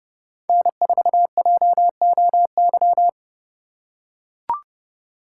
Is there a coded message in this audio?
Yes: Morse code from 0.5 to 3.1 s, heard as a keyed tone, about 29 wpm, 700 Hz. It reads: N4JOY